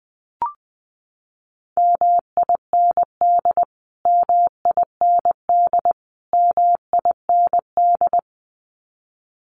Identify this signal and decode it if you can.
Morse code, heard as a keyed tone, about 20 wpm, 700 Hz. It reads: MIND MIND MIND
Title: Top 500 Words Repeated 20wpm